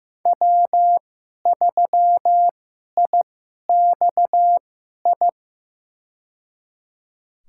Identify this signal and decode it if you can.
Morse code, heard as a keyed tone, about 15 wpm, spaced as for 11 wpm, 700 Hz. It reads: W3IXI
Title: USA Call Signs - 5 Characters - 2x Character Spacing - 15wpm